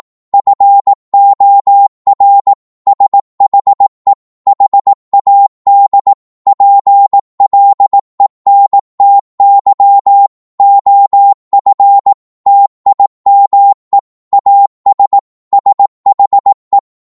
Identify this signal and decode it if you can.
Morse code, heard as a keyed tone, about 18 wpm, 800 Hz. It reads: FOR SHE HAD PLENTY OF TIME AS SHE
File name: Morse / psg-18wpm